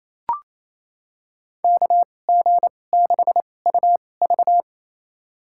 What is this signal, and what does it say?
Morse code, heard as a keyed tone, about 28 wpm, spaced as for 20 wpm, 700 Hz. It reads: KG6UV
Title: USA Call Signs - 5 Characters - 2x Character Spacing - 28wpm